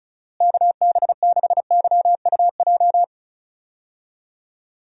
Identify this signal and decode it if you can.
Morse code, heard as a keyed tone, about 36 wpm, 700 Hz. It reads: KB6YUJ
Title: USA Call Signs - 6 Characters - Repeated 2x - 35wpm